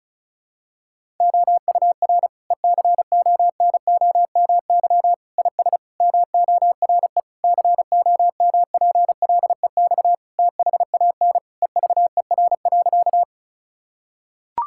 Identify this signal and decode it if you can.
Morse code, heard as a keyed tone, about 36 wpm, 700 Hz. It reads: OUR ECONOMY IS MORE COMPLEX THAN EVER.
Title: Sentences from Top 1000 Words - Review Part 6 - 35wpm